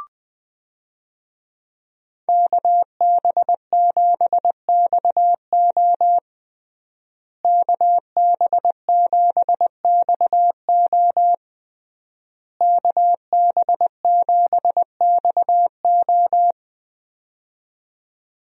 Morse code, heard as a keyed tone, about 20 wpm, 700 Hz. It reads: KB7XO KB7XO KB7XO
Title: USA Call Signs - 5 Characters - Repeated 3x - 20wpm